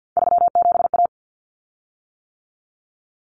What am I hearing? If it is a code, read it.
Morse code, heard as a keyed tone, about 50 wpm, 700 Hz. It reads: 37A
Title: Common Cut Numbers - 3-Digit 50wpm